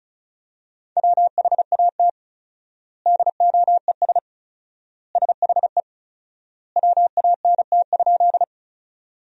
Morse code, heard as a keyed tone, about 34 wpm, 700 Hz. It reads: WHAT DOES SHE WANT?
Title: 4x Word-Spacing - Sentences from Top 200 Words - 35wpm